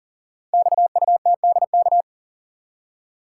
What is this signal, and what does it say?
Morse code, heard as a keyed tone, about 40 wpm, 690 Hz. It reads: XUTDK
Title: Code Groups - 5 Letters - 40wpm